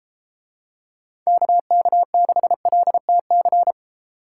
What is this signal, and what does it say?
Morse code, heard as a keyed tone, about 32 wpm, 700 Hz. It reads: KK6LTC